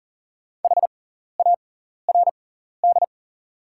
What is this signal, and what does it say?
Morse code, heard as a keyed tone, about 40 wpm, spaced as for 14 wpm, 700 Hz. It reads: HARD
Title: 6x Character Spacing - Top 300 Words - 40wpm